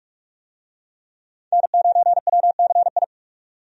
Morse code, heard as a keyed tone, about 46 wpm, 700 Hz. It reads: N9WCI
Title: USA Call Signs - 5 Characters - 45wpm